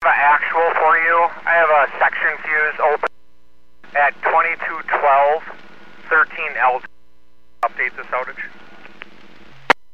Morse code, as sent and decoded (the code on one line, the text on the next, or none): none